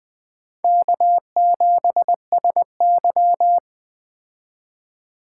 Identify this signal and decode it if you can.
Morse code, heard as a keyed tone, about 21 wpm, 700 Hz. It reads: K7SY
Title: Speed Racing - USA 1x2 Call Signs 17wpm